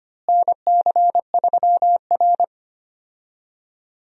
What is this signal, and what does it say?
Morse code, heard as a keyed tone, about 25 wpm, 700 Hz. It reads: NC3R
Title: USA Call Signs - 4 Characters - Repeated 2x - 25wpm